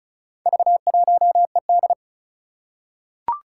Morse code, heard as a keyed tone, about 34 wpm, 700 Hz. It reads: V1ED
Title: Four Challenging Letter-Number Combinations 35wpm